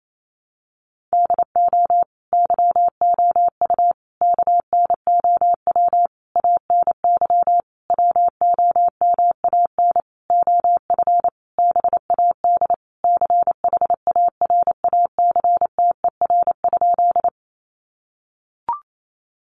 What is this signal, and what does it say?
Morse code, heard as a keyed tone, about 28 wpm, 700 Hz. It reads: DO YOU KNOW ANY WOMAN OF BAD CHARACTER?
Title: Sentences from Top 1200 Words 28wpm